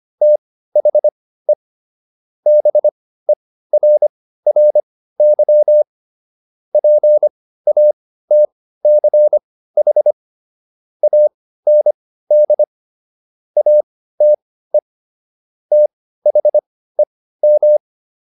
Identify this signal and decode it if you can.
Morse code, heard as a keyed tone, about 25 wpm, spaced as for 15 wpm, 600 Hz. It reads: THE BERRY PATCH AND ATE THEM
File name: SS119 15